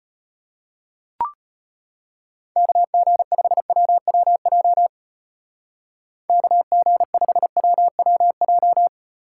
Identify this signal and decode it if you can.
Morse code, heard as a keyed tone, about 32 wpm, 700 Hz. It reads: KG5WWJ KG5WWJ